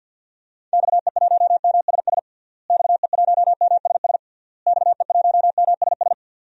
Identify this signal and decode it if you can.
Morse code, heard as a keyed tone, about 52 wpm, 700 Hz. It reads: XE1MSS XE1MSS XE1MSS
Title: DXCC-100 Call Signs - 6 Characters - Repeated 3x - 50wpm